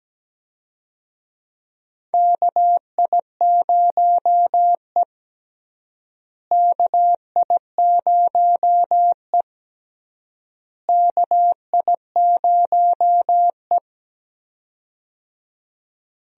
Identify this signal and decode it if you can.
Morse code, heard as a keyed tone, about 17 wpm, 700 Hz. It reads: KI0E KI0E KI0E
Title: USA Call Signs - 4 Characters - Repeated 3x - 17wpm